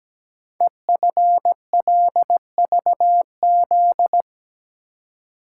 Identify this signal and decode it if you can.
Morse code, heard as a keyed tone, about 17 wpm, 700 Hz. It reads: EFLVZ
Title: Code Groups - 5 Letters - 17wpm